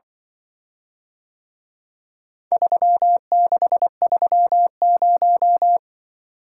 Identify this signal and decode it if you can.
Morse code, heard as a keyed tone, about 24 wpm, 700 Hz. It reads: 3630
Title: Speed-Racing - Four Letter-Number Combinations - 20wpm